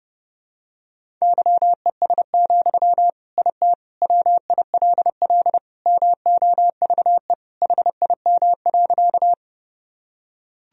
Morse code, heard as a keyed tone, about 30 wpm, 700 Hz. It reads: YES, IT WILL MOVE HIM.
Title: Sentences from Top 1000 Words - Review Part 1 - 30wpm